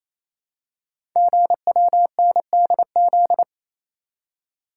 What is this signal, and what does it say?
Morse code, heard as a keyed tone, about 29 wpm, 700 Hz. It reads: GWNDZ
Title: Code Groups - 5 Letters - 28wpm